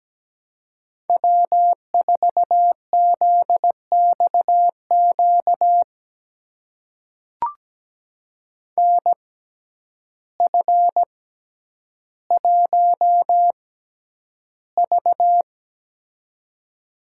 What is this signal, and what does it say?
Morse code, heard as a keyed tone, about 17 wpm, spaced as for 9 wpm, 700 Hz. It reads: W4ZXQ NF1V
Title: USA Call Signs - 6x Character Spacing - 17wpm